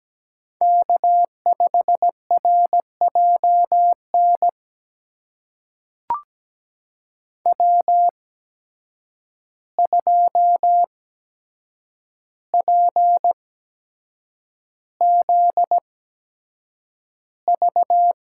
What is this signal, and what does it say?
Morse code, heard as a keyed tone, about 17 wpm, spaced as for 7 wpm, 700 Hz. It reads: K5RJN W2PZV
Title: USA Call Signs - 5 Characters - 8x Character Spacing - 17wpm